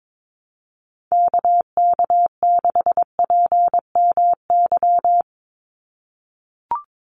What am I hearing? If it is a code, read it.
Morse code, heard as a keyed tone, about 22 wpm, 700 Hz. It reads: KK6PMY
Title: USA Call Signs 22wpm